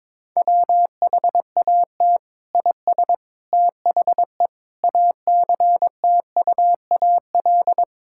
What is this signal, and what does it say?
Morse code, heard as a keyed tone, about 21 wpm, 700 Hz. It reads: WHAT IS THE ACTUAL
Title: Sentences from Top 1000 Words - Review Part 5 - 22wpm